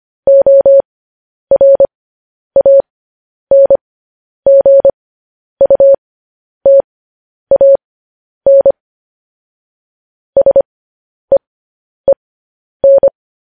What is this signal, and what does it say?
Morse code, heard as a keyed tone, about 25 wpm, spaced as for 10 wpm, 550 Hz. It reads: ORANGUTAN SEEN
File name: podcast / News headlines-1x-25x10